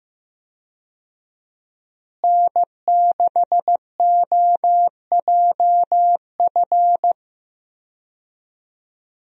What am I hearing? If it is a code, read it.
Morse code, heard as a keyed tone, about 15 wpm, 700 Hz. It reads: N6OJF
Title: USA Call Signs - 5 Characters - Repeated 2x - 15wpm